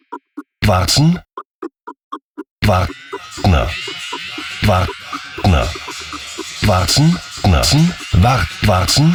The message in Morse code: none

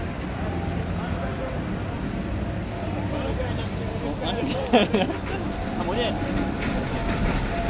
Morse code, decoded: none

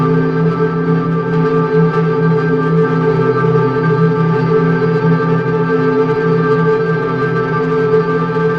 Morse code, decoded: none

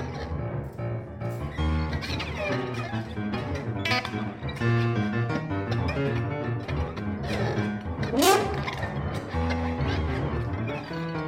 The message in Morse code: none

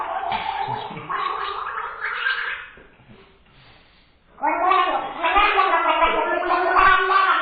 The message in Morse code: none

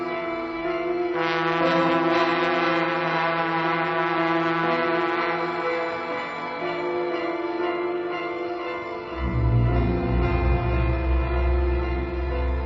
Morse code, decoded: none

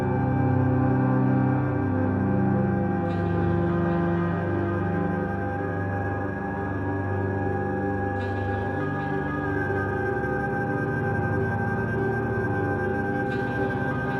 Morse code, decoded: none

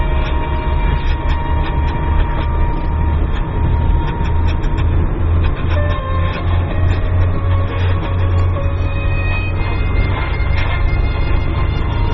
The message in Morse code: none

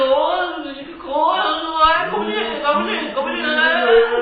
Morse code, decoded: none

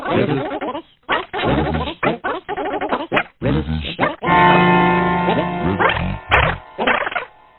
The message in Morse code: none

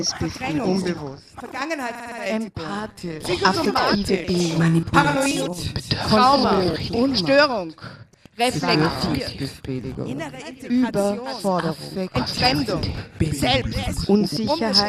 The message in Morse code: none

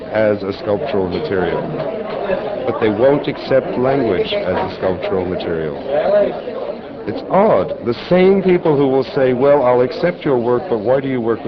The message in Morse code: none

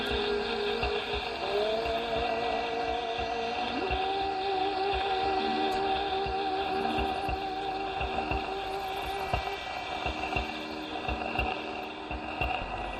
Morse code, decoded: none

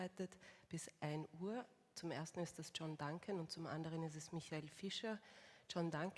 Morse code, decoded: none

